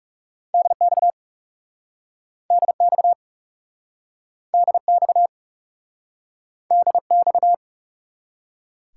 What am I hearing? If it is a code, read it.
Morse code, heard as a keyed tone, about 40 wpm, 700 Hz. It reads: DX DX DX DX